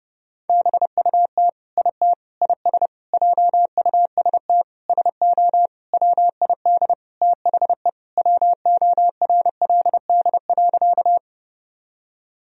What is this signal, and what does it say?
Morse code, heard as a keyed tone, about 30 wpm, 700 Hz. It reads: BUT IT IS JUST SO WID THE WORLD.
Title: Sentences from Top 200 Words and Abbreviations 30wpm